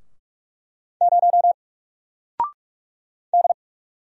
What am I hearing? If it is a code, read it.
Morse code, heard as a keyed tone, about 45 wpm, 700 Hz. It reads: 0 D
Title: Single Letter-Number Alt 45wpm